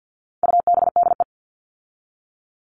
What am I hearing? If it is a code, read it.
Morse code, heard as a keyed tone, about 50 wpm, 700 Hz. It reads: UBDE
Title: All Cut Numbers - 4-Digit 50wpm